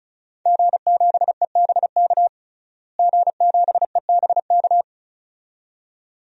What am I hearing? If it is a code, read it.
Morse code, heard as a keyed tone, about 36 wpm, 700 Hz. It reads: G7EBK G7EBK
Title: DXCC-100 Call Signs - 5 Characters - Repeated 2x - 35wpm